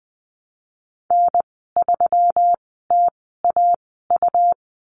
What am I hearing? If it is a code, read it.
Morse code, heard as a keyed tone, about 20 wpm, spaced as for 15 wpm, 700 Hz. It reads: N3TAU
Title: 128 Lesson 21g - Number 6 - Call Signs 20wpm